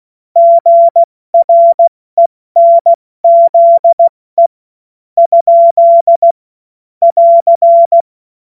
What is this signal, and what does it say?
Morse code, heard as a keyed tone, about 16 wpm, 680 Hz. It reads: GRENZE ? <AR>
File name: Morsekurs / Witze / Witz 26. Oktober 2021, 70 F80